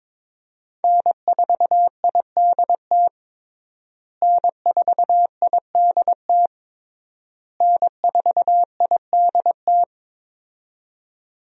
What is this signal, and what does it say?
Morse code, heard as a keyed tone, about 22 wpm, 700 Hz. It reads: N4IDT N4IDT N4IDT
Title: USA Call Signs - 5 Characters - Repeated 3x - 22wpm